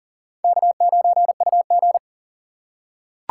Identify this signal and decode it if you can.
Morse code, heard as a keyed tone, about 40 wpm, 700 Hz. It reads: K9UG
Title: USA Call Signs - 2x Character Spacing - 40wpm